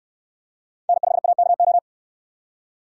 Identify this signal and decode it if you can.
Morse code, heard as a keyed tone, about 68 wpm, 700 Hz. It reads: N5AKO